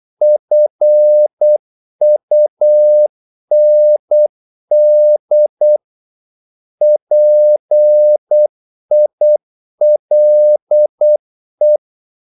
Morse code, heard as a keyed tone, about 8 wpm, 600 Hz. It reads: FUND PILE
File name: Korte ord DK 8wpm